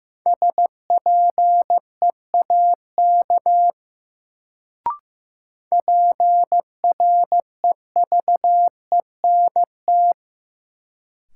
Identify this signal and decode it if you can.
Morse code, heard as a keyed tone, about 15 wpm, 700 Hz. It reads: SPEAK PREVENT
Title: Sentences from Top 1100 Words - New Word Drill Alt - 15wpm